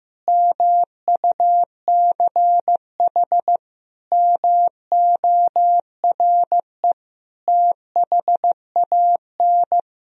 Morse code, heard as a keyed tone, about 15 wpm, 700 Hz. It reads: MUCH MORE THAN
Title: Sentences from Top 1000 Words - Review Part 1 - 15wpm